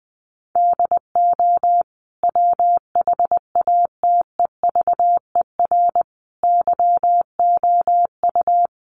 Morse code, heard as a keyed tone, about 20 wpm, 700 Hz. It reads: DO WHATEVER YOU
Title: Sentences from Top 1100 Words 20wpm